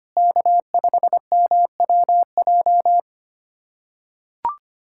Morse code, heard as a keyed tone, about 25 wpm, 700 Hz. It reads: K5MWJ
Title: USA Callsign DE Callsign - 5 Characters - 25wpm